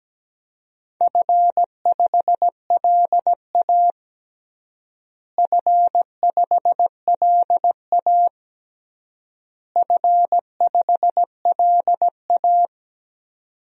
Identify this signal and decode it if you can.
Morse code, heard as a keyed tone, about 17 wpm, 700 Hz. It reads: F5LA F5LA F5LA
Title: DXCC-100 Call Signs - 4 Characters - Repeated 3x - 17wpm